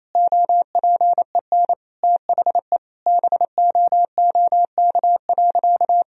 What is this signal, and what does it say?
Morse code, heard as a keyed tone, about 28 wpm, 700 Hz. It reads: OPEN THE BOOK.
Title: Sentences from Top 1000 Words - Review Part 1 - 28wpm